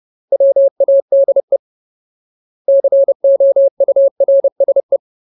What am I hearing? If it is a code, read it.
Morse code, heard as a keyed tone, about 30 wpm, 550 Hz. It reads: WADE COURSE
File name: Ultimate Word List / 30WPM Ultimate